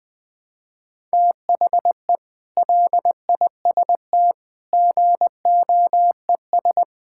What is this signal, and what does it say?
Morse code, heard as a keyed tone, about 20 wpm, 700 Hz. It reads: THE LIST GOES
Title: Sets of 6 Words - 20wpm